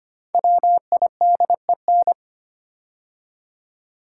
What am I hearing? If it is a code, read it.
Morse code, heard as a keyed tone, about 25 wpm, 700 Hz. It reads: WIDEN
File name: 5 Letter Words - First 3 Letters Given - 25wpm